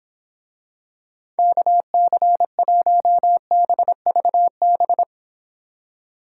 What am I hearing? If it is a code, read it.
Morse code, heard as a keyed tone, about 25 wpm, 700 Hz. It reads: KC1BVB